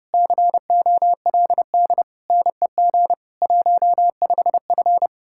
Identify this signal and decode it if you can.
Morse code, heard as a keyed tone, about 30 wpm, 700 Hz. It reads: COLD NEG 15F